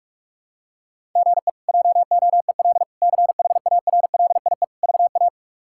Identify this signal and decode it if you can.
Morse code, heard as a keyed tone, about 46 wpm, 700 Hz. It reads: GE JOEL CHARLIE VA
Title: K1USN SST QSO Practice - 45wpm